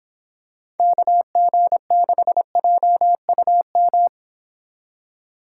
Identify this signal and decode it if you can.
Morse code, heard as a keyed tone, about 26 wpm, 700 Hz. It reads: KG6JUM